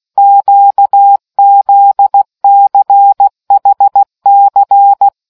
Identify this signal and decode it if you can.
Morse code, heard as a keyed tone, about 16 wpm, 800 Hz. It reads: QZCHC